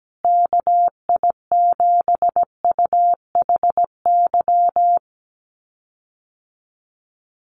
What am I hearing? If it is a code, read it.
Morse code, heard as a keyed tone, about 17 wpm, 700 Hz. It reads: KI7UHY
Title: USA Call Signs Repeated 2x 17wpm